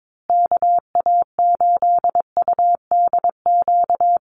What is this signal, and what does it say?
Morse code, heard as a keyed tone, about 22 wpm, 700 Hz. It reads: KA8UDQ
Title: USA Call Signs Repeated 3x 22wpm